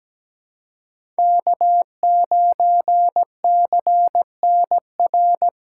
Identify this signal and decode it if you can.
Morse code, heard as a keyed tone, about 17 wpm, 700 Hz. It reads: K9CNR
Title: USA Call Signs - 5 Characters - 3x Character Spacing - 17wpm